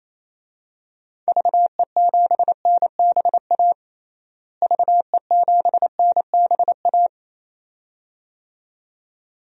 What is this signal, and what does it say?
Morse code, heard as a keyed tone, about 28 wpm, 700 Hz. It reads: VE7NBA VE7NBA
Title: DXCC-100 Call Signs - 6 Characters - Repeated 2x - 28wpm